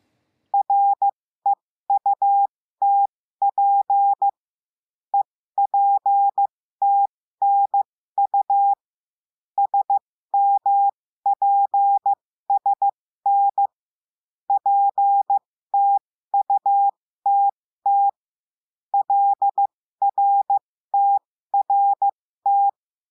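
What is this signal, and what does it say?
Morse code, heard as a keyed tone, about 15 wpm, spaced as for 13 wpm, 800 Hz. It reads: REUTP EPTNU SMPSN PTUTT LRTRT
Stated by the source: Morse lesson adding the letter L, five-letter random groups.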